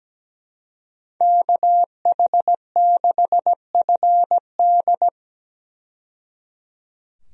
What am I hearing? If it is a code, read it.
Morse code, heard as a keyed tone, about 17 wpm, 700 Hz. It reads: KH6FD
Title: USA Call Signs - 5 Characters - 17wpm